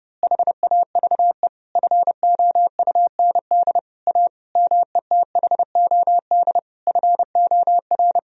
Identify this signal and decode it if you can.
Morse code, heard as a keyed tone, about 30 wpm, 700 Hz. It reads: HAVE FOUND A METHOD FOR
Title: Sentences from Top 1000 Words - Review Part 3 - 30wpm